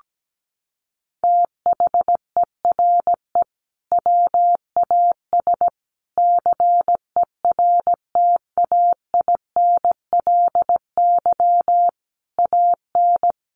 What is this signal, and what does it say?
Morse code, heard as a keyed tone, about 17 wpm, 700 Hz. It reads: THERE WAS CERTAINLY AN
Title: Sentences from Top 1400 Words 17wpm